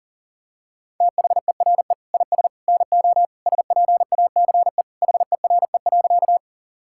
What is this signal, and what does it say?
Morse code, heard as a keyed tone, about 40 wpm, 700 Hz. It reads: THERE IS NO SPACE HERE.
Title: Sentences from Top 1000 Words - Review Part 5 - 40wpm